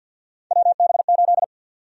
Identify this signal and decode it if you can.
Morse code, heard as a keyed tone, about 48 wpm, 700 Hz. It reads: WB8
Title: USA Call Sign Prefixes - Repeated 2x - 50wpm